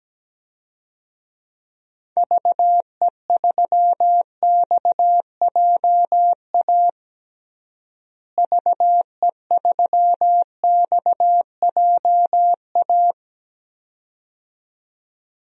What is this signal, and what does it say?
Morse code, heard as a keyed tone, about 17 wpm, 700 Hz. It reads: VE3XJA VE3XJA
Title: DXCC-100 Call Signs - 6 Characters - Repeated 2x - 17wpm